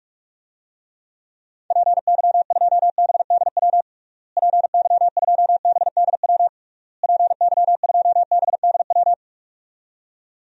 Morse code, heard as a keyed tone, about 46 wpm, 700 Hz. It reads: PY2BDW PY2BDW PY2BDW
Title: DXCC-100 Call Signs - 6 Characters - Repeated 3x - 45wpm